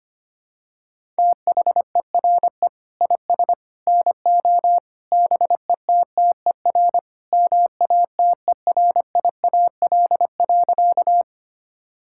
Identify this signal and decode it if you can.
Morse code, heard as a keyed tone, about 25 wpm, 700 Hz. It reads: THERE IS NO BETTER MATERIAL.
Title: Sentences from Top 1000 Words - Review Part 5 - 25wpm